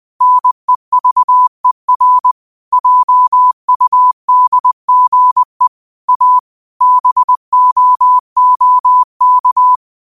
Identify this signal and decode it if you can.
Morse code, heard as a keyed tone, about 20 wpm, 1000 Hz. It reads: NEVER JUDGE A BOOK
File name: qotdfiles / 20wpm